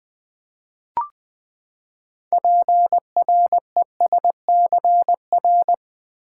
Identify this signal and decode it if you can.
Morse code, heard as a keyed tone, about 20 wpm, 700 Hz. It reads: PRESCR
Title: Build A Word - 6 Letters At A Time - 12-Letter Words - 20wpm